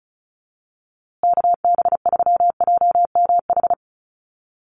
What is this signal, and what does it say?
Morse code, heard as a keyed tone, about 34 wpm, 700 Hz. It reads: KB3JMH
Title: USA Call Signs Repeated 3x 35wpm